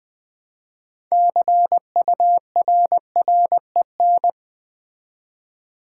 Morse code, heard as a keyed tone, about 20 wpm, 700 Hz. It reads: CURREN
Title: Build A Word - 3 Letters At A Time - 8-Letter Words - 20wpm